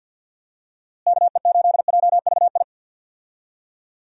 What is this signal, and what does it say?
Morse code, heard as a keyed tone, about 46 wpm, 700 Hz. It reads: KE8JVI
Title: USA Call Signs - 6 Characters - Repeated 2x - 50wpm